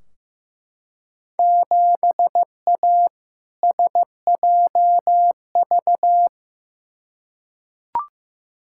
Morse code, heard as a keyed tone, about 15 wpm, 700 Hz. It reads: 7A SJV
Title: Field Day Exchanges Encore - 15wpm